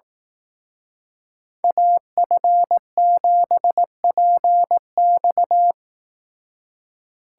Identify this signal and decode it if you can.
Morse code, heard as a keyed tone, about 18 wpm, 700 Hz. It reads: AF7PX
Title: Speed Racing - USA 2x2 Call Signs 15wpm